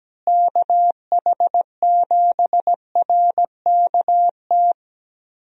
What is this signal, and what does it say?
Morse code, heard as a keyed tone, about 17 wpm, 700 Hz. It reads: KH7RKT
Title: USA Call Signs - 6 Characters - Repeated 3x - 17wpm